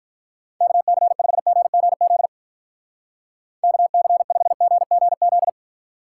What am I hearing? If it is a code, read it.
Morse code, heard as a keyed tone, about 44 wpm, 700 Hz. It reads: KC5GGZ KC5GGZ
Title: Speed Racing - USA 2x3 Call Signs 35wpm